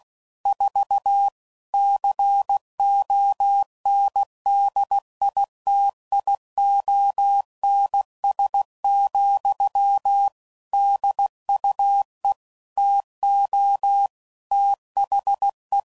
Morse code, heard as a keyed tone, about 16 wpm, 800 Hz. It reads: 4 CONDITIONS, DUE TO THE
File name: CW Code Groups @15WPM / Technical Subjects / AND Gate